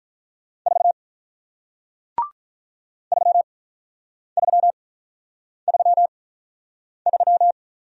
Morse code, heard as a keyed tone, about 45 wpm, spaced as for 9 wpm, 700 Hz. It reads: 4 3333